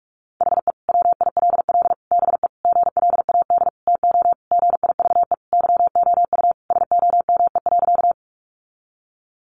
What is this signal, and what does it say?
Morse code, heard as a keyed tone, about 44 wpm, 700 Hz. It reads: HE WILL BE GLAD TO GIVE YOU SOME.
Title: Sentences from Top 800 Words 45wpm